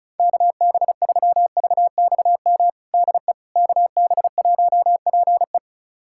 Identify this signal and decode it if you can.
Morse code, heard as a keyed tone, about 34 wpm, 700 Hz. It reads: KB3VXM DE KB1PE
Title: USA Callsign DE Callsign 35wpm